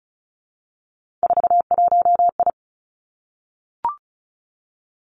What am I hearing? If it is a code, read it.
Morse code, heard as a keyed tone, about 36 wpm, 700 Hz. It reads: 41I
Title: Three Challenging Letter-Number Combinations 35wpm